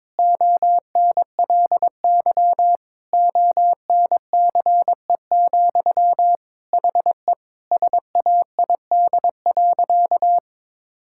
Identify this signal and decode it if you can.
Morse code, heard as a keyed tone, about 21 wpm, 700 Hz. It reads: ONLY ONCE, HE SAID.
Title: Sentences from Top 1600 Words 22wpm